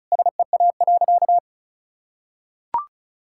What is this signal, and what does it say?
Morse code, heard as a keyed tone, about 36 wpm, 690 Hz. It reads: SEA.
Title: Sentences from Top 1000 Words - Review Part 5 - 35wpm